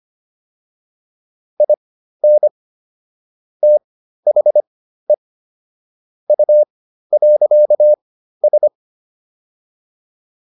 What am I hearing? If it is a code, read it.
Morse code, heard as a keyed tone, about 25 wpm, spaced as for 13 wpm, 600 Hz. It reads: IN THE U.S